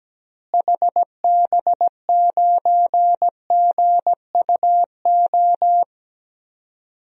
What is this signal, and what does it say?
Morse code, heard as a keyed tone, about 17 wpm, 700 Hz. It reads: HB9GUO
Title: DXCC-100 Call Signs - 6 Characters - Repeated 3x - 17wpm